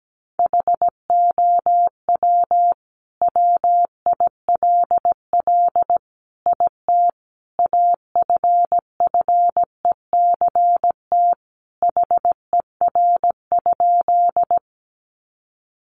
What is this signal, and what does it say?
Morse code, heard as a keyed tone, about 17 wpm, 700 Hz. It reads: HOW WILL IT AFFECT HER?
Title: Sentences from Top 900 Words 17wpm